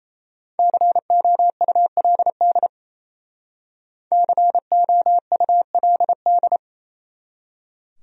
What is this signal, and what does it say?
Morse code, heard as a keyed tone, about 36 wpm, 700 Hz. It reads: COULD COULD